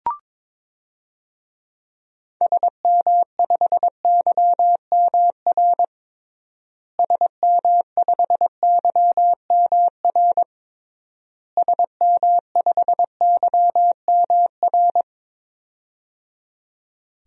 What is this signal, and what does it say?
Morse code, heard as a keyed tone, about 22 wpm, 700 Hz. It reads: SM5YMR SM5YMR SM5YMR
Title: DXCC-100 Call Signs - 6 Characters - Repeated 3x - 22wpm